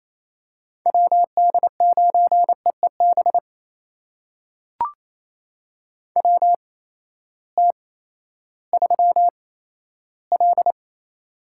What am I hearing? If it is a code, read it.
Morse code, heard as a keyed tone, about 27 wpm, spaced as for 14 wpm, 700 Hz. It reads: WD9EEB WT3L
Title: USA Call Signs - 8x Character Spacing - 28wpm